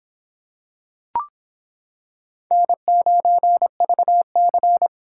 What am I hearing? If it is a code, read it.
Morse code, heard as a keyed tone, about 25 wpm, 700 Hz. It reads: N9VC